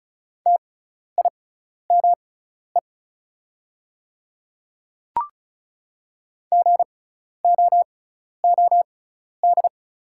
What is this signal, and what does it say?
Morse code, heard as a keyed tone, about 36 wpm, spaced as for 12 wpm, 700 Hz. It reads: TIME GOOD
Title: 6x Character Spacing - Top 200 Words - 35wpm